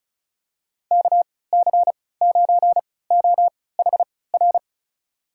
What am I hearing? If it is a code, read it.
Morse code, heard as a keyed tone, about 34 wpm, spaced as for 20 wpm, 700 Hz. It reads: KC9OHR
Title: USA Call Signs - 3x Character Spacing - 35wpm